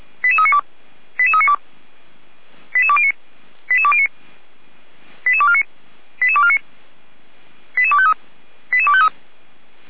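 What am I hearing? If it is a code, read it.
Five-tone selective call (ZVEI), heard as quick runs of five tones, tones of about 70 ms. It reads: 88392 88392 88209 88209 88249 88249 88253 88253
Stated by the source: Received on an SDR, ZVEI five-tone code 88149.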